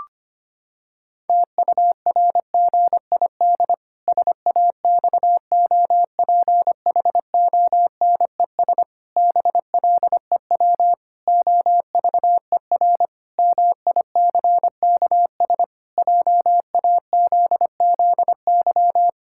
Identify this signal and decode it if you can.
Morse code, heard as a keyed tone, about 25 wpm, 700 Hz. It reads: TURGID SAXOPHONES BLEW OVER MICKS JAZZY
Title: Pangram Challenge v2 - 25wpm